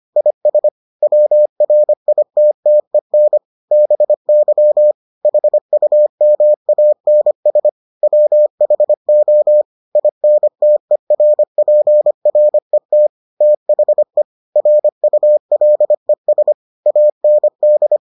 Morse code, heard as a keyed tone, about 25 wpm, 600 Hz. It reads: IS WRITTEN BY HUMANS WHO INTERPRET THE RULES AND